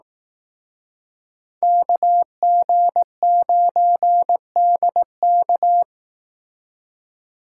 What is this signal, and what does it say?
Morse code, heard as a keyed tone, about 18 wpm, 700 Hz. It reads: KG9DK
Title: Speed Racing - USA 2x2 Call Signs 15wpm